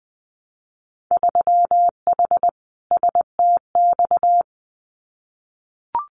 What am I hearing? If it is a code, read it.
Morse code, heard as a keyed tone, about 20 wpm, 700 Hz. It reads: 3H STX
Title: Winter Field Day Exchanges 20wpm